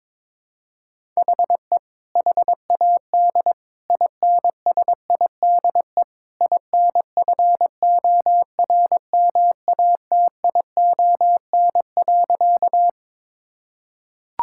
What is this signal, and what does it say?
Morse code, heard as a keyed tone, about 21 wpm, 700 Hz. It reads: HE HAD INSIDE INFORMATION.
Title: Sentences from Top 1000 Words - Review Part 6 - 22wpm